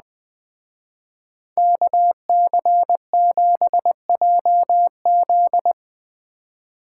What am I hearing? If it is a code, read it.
Morse code, heard as a keyed tone, about 20 wpm, 700 Hz. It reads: KC7JZ